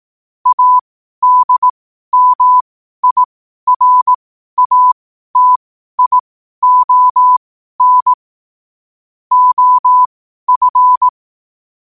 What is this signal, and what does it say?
Morse code, heard as a keyed tone, about 18 wpm, spaced as for 12 wpm, 1000 Hz. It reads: ADMIRATION OF
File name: qotdfiles / 12wpm